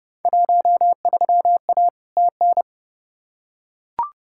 Morse code, heard as a keyed tone, about 30 wpm, 700 Hz. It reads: 13A TN
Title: Field Day Exchanges Encore - 30wpm